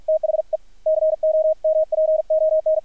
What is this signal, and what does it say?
Morse code, heard as a keyed tone, about 53 wpm, 630 Hz. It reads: THE COMPON
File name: archived / CRSnet